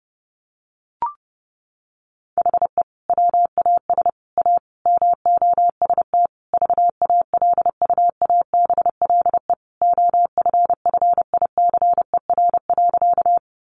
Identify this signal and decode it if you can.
Morse code, heard as a keyed tone, about 30 wpm, 700 Hz. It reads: HE WAS A MOST VALUABLE OFFICER.
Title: Sentences from Top 1200 Words 30wpm